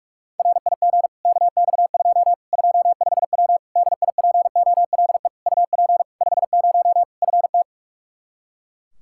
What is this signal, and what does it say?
Morse code, heard as a keyed tone, about 41 wpm, 700 Hz. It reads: RIG KX2 25W DIPOLE UP 50 FT
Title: K1USN SST QSO Practice 45wpm